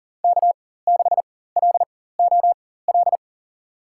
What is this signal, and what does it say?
Morse code, heard as a keyed tone, about 40 wpm, spaced as for 19 wpm, 700 Hz. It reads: K6LOL
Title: USA Call Signs - 5 Characters - 4x Character Spacing - 40wpm